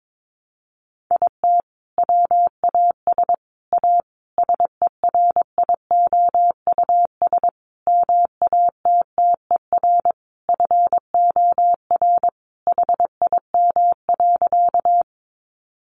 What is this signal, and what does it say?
Morse code, heard as a keyed tone, about 22 wpm, 700 Hz. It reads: IT WAS A SERIOUS MATTER FOR HIM.
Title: Sentences from Top 1100 Words 22wpm